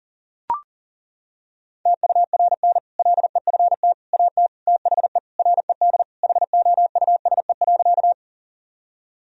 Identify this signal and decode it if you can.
Morse code, heard as a keyed tone, about 40 wpm, 700 Hz. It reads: TURN LEFT AT THE RED HOUSE.
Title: Sentences from Top 1000 Words - Review Part 2 - 40wpm